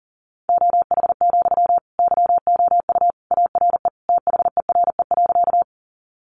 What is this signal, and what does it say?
Morse code, heard as a keyed tone, about 40 wpm, 700 Hz. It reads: OH, YOU ARE THERE.
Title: Sentences from Top 500 Words 40wpm